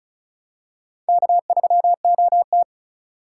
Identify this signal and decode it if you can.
Morse code, heard as a keyed tone, about 34 wpm, 700 Hz. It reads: K3OT
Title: USA Call Signs - 4 Characters - 2x Character Spacing - 35wpm